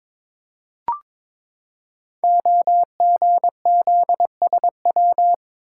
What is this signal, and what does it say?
Morse code, heard as a keyed tone, about 22 wpm, 700 Hz. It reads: OGZSW
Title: Code Groups - 5 Letter:Numbers - 22wpm